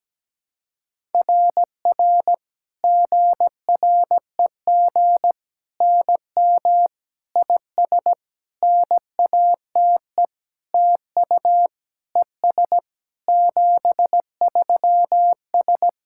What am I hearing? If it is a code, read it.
Morse code, heard as a keyed tone, about 17 wpm, 700 Hz. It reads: RR GREG NM IS NATE TU ES 73S
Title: SOTA QSO Practice 17wpm